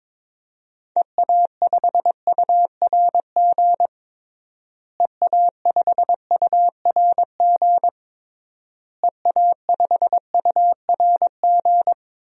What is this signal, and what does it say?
Morse code, heard as a keyed tone, about 22 wpm, 700 Hz. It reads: EA5URG EA5URG EA5URG
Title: DXCC-100 Call Signs - 6 Characters - Repeated 3x - 22wpm